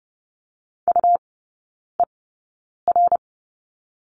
Morse code, heard as a keyed tone, about 30 wpm, spaced as for 9 wpm, 690 Hz. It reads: UER
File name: Lesson 17e - Letter u - Three Characters 30wpm